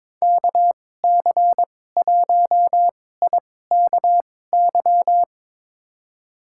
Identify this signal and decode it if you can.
Morse code, heard as a keyed tone, about 23 wpm, spaced as for 16 wpm, 700 Hz. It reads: KC1IKY
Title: USA Call Signs - 6 Characters - 2x Character Spacing - 22wpm